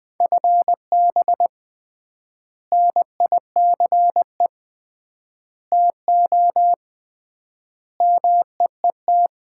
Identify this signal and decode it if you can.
Morse code, heard as a keyed tone, about 21 wpm, 700 Hz. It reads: FB NICE TO MEET
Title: Common QSO Phrases - 3x Word-Spacing - 20wpm